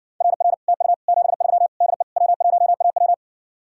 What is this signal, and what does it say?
Morse code, heard as a keyed tone, about 68 wpm, 700 Hz. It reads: RR TU 73 DE W1AW